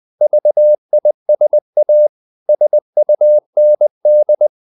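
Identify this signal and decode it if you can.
Morse code, heard as a keyed tone, about 20 wpm, 600 Hz. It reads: VISA SUND
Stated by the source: Morse code practice of short Danish words at 20 wpm, from CW med Gnister.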